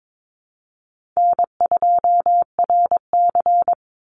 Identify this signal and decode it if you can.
Morse code, heard as a keyed tone, about 22 wpm, 700 Hz. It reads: N2RC
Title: Easy Call Signs Repeated 2x - 22wpm